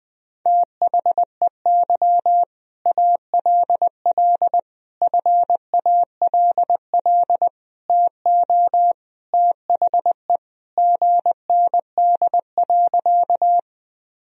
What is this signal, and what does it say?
Morse code, heard as a keyed tone, about 20 wpm, 700 Hz. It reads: THEY ALL FALL TO THE GND.
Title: Sentences from Top 500 Words and Abbreviations 20wpm